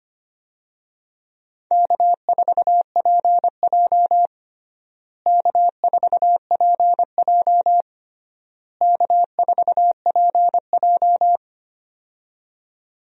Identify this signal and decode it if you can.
Morse code, heard as a keyed tone, about 25 wpm, 700 Hz. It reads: K4PJ K4PJ K4PJ
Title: USA Call Signs - 4 Characters - Repeated 3x - 25wpm